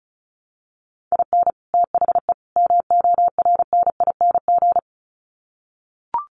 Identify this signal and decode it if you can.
Morse code, heard as a keyed tone, about 34 wpm, 700 Hz. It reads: IN THE MORNING
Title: Sets of 3 Words 35wpm